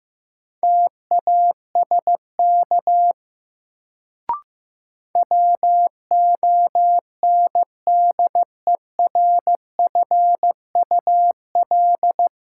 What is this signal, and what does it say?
Morse code, heard as a keyed tone, about 15 wpm, 700 Hz. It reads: TASK WONDERFUL
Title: Sentences from Top 1100 Words - New Word Drill Alt - 15wpm